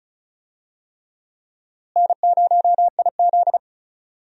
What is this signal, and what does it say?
Morse code, heard as a keyed tone, about 36 wpm, 700 Hz. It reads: N0IZ
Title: USA Call Signs - 4 Characters - Repeated 3x - 35wpm